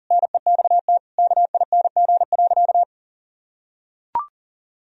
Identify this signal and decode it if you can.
Morse code, heard as a keyed tone, about 40 wpm, 700 Hz. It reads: NEXT KING.
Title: Sentences from Top 1000 Words - Review Part 3 - 40wpm